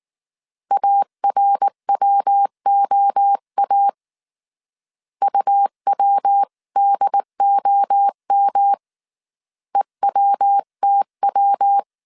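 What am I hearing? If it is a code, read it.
Morse code, heard as a keyed tone, about 19 wpm, 800 Hz. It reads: ARWOA UWDOM EWTW
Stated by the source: Morse code dictation at 18 wpm from the IW3SGT QRS Morse code training course, five-character groups of 300 characters.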